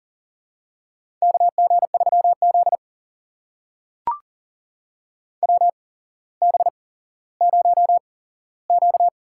Morse code, heard as a keyed tone, about 40 wpm, spaced as for 17 wpm, 700 Hz. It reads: KG3Z WB0Q
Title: USA Call Signs - 4 Characters - 8x Character Spacing - 40wpm